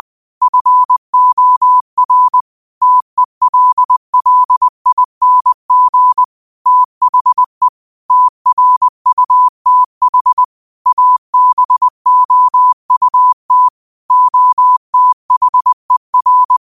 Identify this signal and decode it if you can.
Morse code, heard as a keyed tone, about 20 wpm, 1000 Hz. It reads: FOR TELLING THE TRUTH ABOUT OTHER
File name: qotdfiles / 20wpm